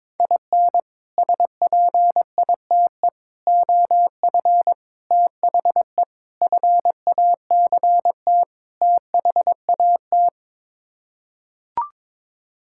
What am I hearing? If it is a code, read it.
Morse code, heard as a keyed tone, about 21 wpm, 700 Hz. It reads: IN SPITE OF THE FACT THAT
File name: practice / 22wpm / Sets of 6 Words - 22wpm